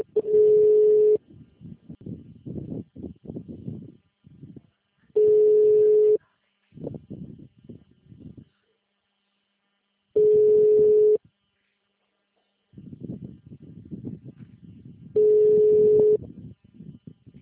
Ringback tone (Europe, 425 Hz).
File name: out- -2010- - - .28293